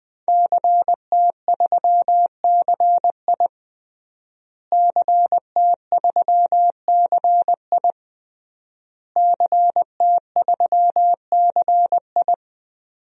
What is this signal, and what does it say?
Morse code, heard as a keyed tone, about 20 wpm, 700 Hz. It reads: CT3CI CT3CI CT3CI
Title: DXCC-100 Call Signs - 5 Characters - Repeated 3x - 20wpm